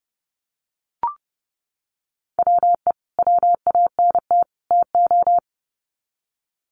Morse code, heard as a keyed tone, about 30 wpm, 700 Hz. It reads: WE WANT TO
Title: Sets of 3 Words 30wpm